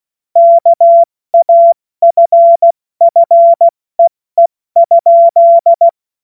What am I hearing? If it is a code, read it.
Morse code, heard as a keyed tone, about 16 wpm, 680 Hz. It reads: KAFFEE?
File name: Morsekurs / Witze / Witz 23. Mai 2023 70 F80